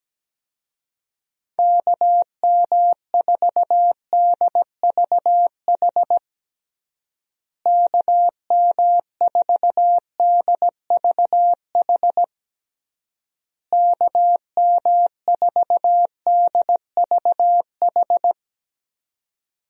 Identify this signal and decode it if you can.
Morse code, heard as a keyed tone, about 17 wpm, 700 Hz. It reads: KM4DVH KM4DVH KM4DVH
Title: USA Call Signs - 6 Characters - Repeated 3x - 17wpm